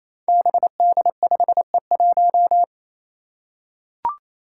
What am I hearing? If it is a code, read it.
Morse code, heard as a keyed tone, about 27 wpm, 700 Hz. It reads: BD5E1
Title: Five Challenging Letter-Number Combinations 28wpm